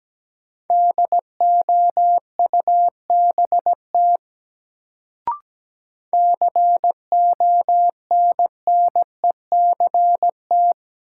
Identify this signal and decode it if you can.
Morse code, heard as a keyed tone, about 17 wpm, 700 Hz. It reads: DOUBT CONNECT
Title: Sentences from Top 1500 Words - New Word Drill - 17wpm